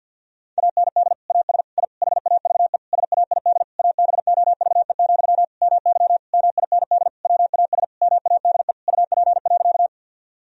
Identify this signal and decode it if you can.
Morse code, heard as a keyed tone, about 48 wpm, 700 Hz. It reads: AND AS I HAVE SAID ABOVE, MY MIND WAS MADE UP.
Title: Sentences from Top 1000 Words - Review Part 2 - 50wpm